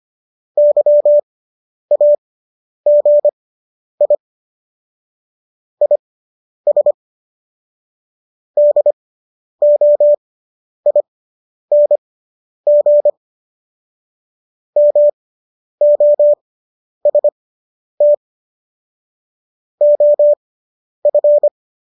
Morse code, heard as a keyed tone, about 25 wpm, spaced as for 10 wpm, 590 Hz. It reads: YAGI IS DOING MOST OF